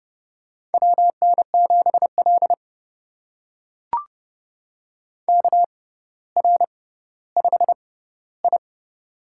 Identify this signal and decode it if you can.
Morse code, heard as a keyed tone, about 30 wpm, spaced as for 15 wpm, 700 Hz. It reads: WN7L KR5I
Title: USA Call Signs - 4 Characters - 6x Character Spacing - 30wpm